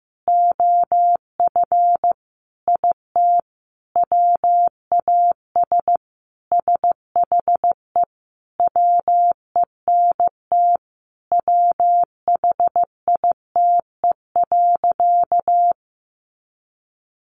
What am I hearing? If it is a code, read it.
Morse code, heard as a keyed tone, about 15 wpm, 700 Hz. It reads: OF IT WAS SHE WENT WHITE.
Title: Sentences from Top 1200 Words 15wpm